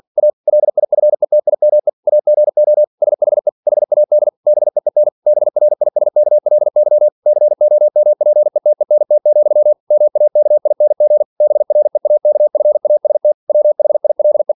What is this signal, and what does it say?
Morse code, heard as a keyed tone, about 47 wpm, 600 Hz. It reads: A LIFETIME AGO SHE HAD BEEN BRISKLY COMPETENT, MAKING BREAKFAST WHILE
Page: Never 47wpm